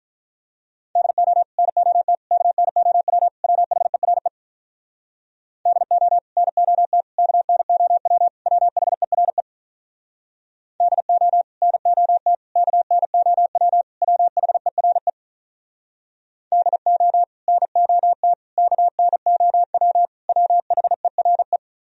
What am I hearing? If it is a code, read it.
Morse code, heard as a keyed tone, about 45 wpm, 700 Hz. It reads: DO NOT KNOW WHERE DO NOT KNOW WHERE DO NOT KNOW WHERE DO NOT KNOW WHERE